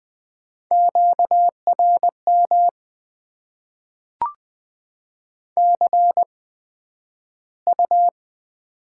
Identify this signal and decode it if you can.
Morse code, heard as a keyed tone, about 20 wpm, spaced as for 11 wpm, 700 Hz. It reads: QRM CU
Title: Top 90 QSO Elements - 8x Character Spacing - 20wpm